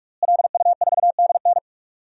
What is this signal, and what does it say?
Morse code, heard as a keyed tone, about 46 wpm, 700 Hz. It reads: LU4DN
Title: DXCC-100 Call Signs - 5 Characters - Repeated 3x - 45wpm